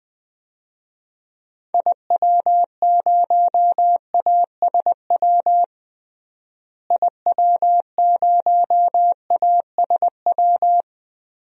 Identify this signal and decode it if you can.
Morse code, heard as a keyed tone, about 20 wpm, 700 Hz. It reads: IW0ASW IW0ASW
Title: DXCC-100 Call Signs - 6 Characters - Repeated 2x - 20wpm